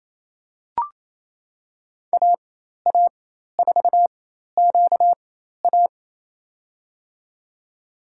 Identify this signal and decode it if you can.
Morse code, heard as a keyed tone, about 27 wpm, spaced as for 13 wpm, 700 Hz. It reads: AA4QA
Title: USA Call Signs - 5 Characters - 4x Character Spacing - 28wpm